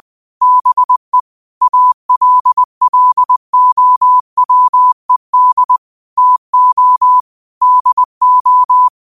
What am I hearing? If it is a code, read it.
Morse code, heard as a keyed tone, about 20 wpm, 1000 Hz. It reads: BE ALLOWED TO DO
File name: qotdfiles / 20wpm